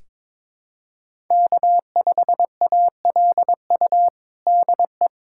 Morse code, heard as a keyed tone, about 22 wpm, 700 Hz. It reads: K5ALU DE
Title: USA Callsign DE Callsign - 5 Characters - 22wpm